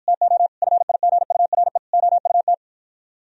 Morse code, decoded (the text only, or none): TO FIGURE OUT